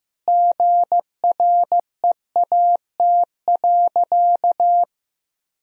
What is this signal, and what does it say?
Morse code, heard as a keyed tone, about 15 wpm, 700 Hz. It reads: GREAT.
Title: Sentences from Top 1000 Words - Review Part 1 - 15wpm